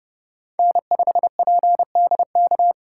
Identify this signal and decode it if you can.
Morse code, heard as a keyed tone, about 31 wpm, 700 Hz. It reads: N5PDK
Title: USA Call Signs - 5 Characters - 6x Character Spacing - 30wpm